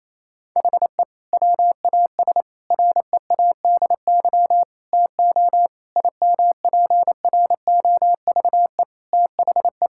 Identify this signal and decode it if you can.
Morse code, heard as a keyed tone, about 28 wpm, 700 Hz. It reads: HE WAS READY TO IMPROVE THE
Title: Sentences from Top 1000 Words - Review Part 5 - 28wpm